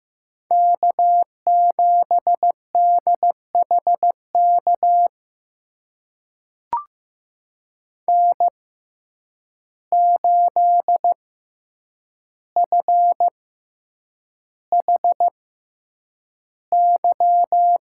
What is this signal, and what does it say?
Morse code, heard as a keyed tone, about 15 wpm, spaced as for 8 wpm, 700 Hz. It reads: K7DHK N8FHY
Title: USA Call Signs - 5 Characters - 6x Character Spacing - 15wpm